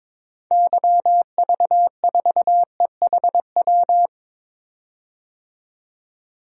Morse code, heard as a keyed tone, about 23 wpm, 700 Hz. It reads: YV4EHW
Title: DXCC-100 Call Signs - 6 Characters - 22wpm